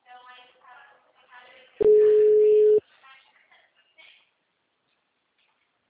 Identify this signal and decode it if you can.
Ringback tone (Europe, 425 Hz), with speech.
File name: out- -2005- - - .86399